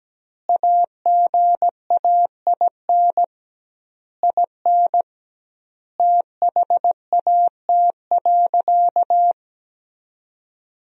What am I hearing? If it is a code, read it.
Morse code, heard as a keyed tone, about 17 wpm, 700 Hz. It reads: AGAIN IN THAT.
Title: 2x Word-Spacing - Sentences from Top 200 Words - 17wpm